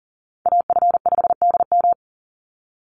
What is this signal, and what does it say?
Morse code, heard as a keyed tone, about 40 wpm, 700 Hz. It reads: AF5DM
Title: Call Signs - 40wpm